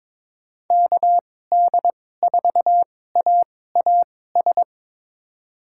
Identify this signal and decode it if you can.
Morse code, heard as a keyed tone, about 22 wpm, spaced as for 16 wpm, 700 Hz. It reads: KD4AAS
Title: USA Call Signs - 6 Characters - 2x Character Spacing - 22wpm